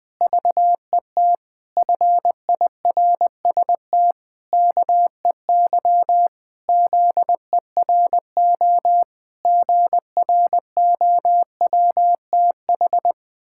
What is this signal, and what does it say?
Morse code, heard as a keyed tone, about 20 wpm, 700 Hz. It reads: VET FIRST KEY ZERO GROWTH
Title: Pangram Challenge v2 - 20wpm